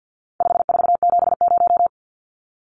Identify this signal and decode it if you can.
Morse code, heard as a keyed tone, about 50 wpm, 700 Hz. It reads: 5470
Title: Numbers - 4-Digit 50wpm